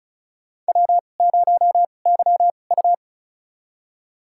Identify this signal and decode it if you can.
Morse code, heard as a keyed tone, about 34 wpm, spaced as for 25 wpm, 700 Hz. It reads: W0YU
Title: USA Call Signs - 4 Characters - 2x Character Spacing - 35wpm